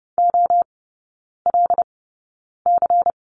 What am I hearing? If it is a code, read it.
Morse code, heard as a keyed tone, about 30 wpm, spaced as for 9 wpm, 690 Hz. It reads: OLC
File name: Lesson 16e - Letter c - Three Characters 30wpm